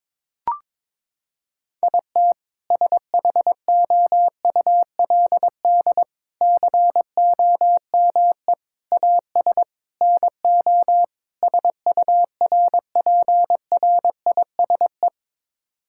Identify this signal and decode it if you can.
Morse code, heard as a keyed tone, about 22 wpm, 700 Hz. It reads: IT SHOULD COME AS NO SURPRISE
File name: practice / 22wpm / Sets of 6 Words - 22wpm